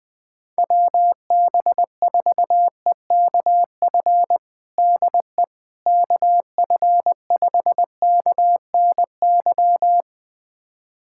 Morse code, heard as a keyed tone, about 19 wpm, 700 Hz. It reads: WB4EKF DE KF5KNY
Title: USA Callsign DE Callsign - 6 Characters - 20wpm